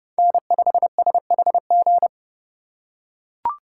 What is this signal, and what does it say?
Morse code, heard as a keyed tone, about 30 wpm, 700 Hz. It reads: N5SHG
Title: USA Call Signs - 5 Characters - 3x Character Spacing - 30wpm